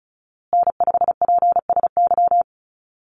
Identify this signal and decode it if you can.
Morse code, heard as a keyed tone, about 34 wpm, 700 Hz. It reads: N5PSY
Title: USA Call Signs Repeated 2x 35wpm